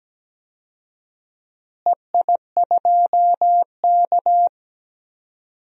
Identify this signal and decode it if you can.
Morse code, heard as a keyed tone, about 17 wpm, 700 Hz. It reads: EI2K
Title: DXCC-100 Call Signs - 4 Characters - Repeated 3x - 17wpm